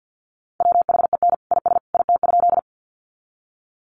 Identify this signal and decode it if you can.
Morse code, heard as a keyed tone, about 50 wpm, 700 Hz. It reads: WHEN IS IT?
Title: Sentences from Top 100 Words 50wpm